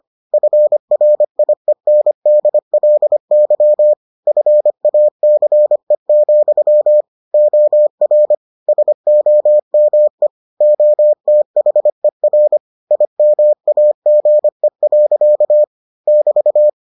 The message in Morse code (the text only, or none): FRIENDLY FACE, OR SOME OTHER IMAGE. <BT>